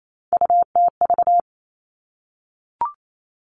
Morse code, heard as a keyed tone, about 28 wpm, 700 Hz. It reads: UTV
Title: Three Challenging Letter-Number Combinations 28wpm